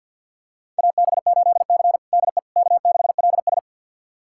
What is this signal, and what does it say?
Morse code, heard as a keyed tone, about 52 wpm, 700 Hz. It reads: AB8C DE K6LS
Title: USA Callsign DE Callsign - 4 Characters - 50wpm